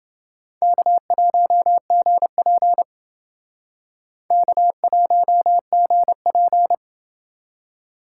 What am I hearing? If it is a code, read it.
Morse code, heard as a keyed tone, about 29 wpm, 700 Hz. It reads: K1GP K1GP